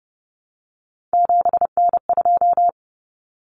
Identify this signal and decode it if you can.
Morse code, heard as a keyed tone, about 30 wpm, 700 Hz. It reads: 7N2
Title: Common Cut Numbers - 3-Digit 30wpm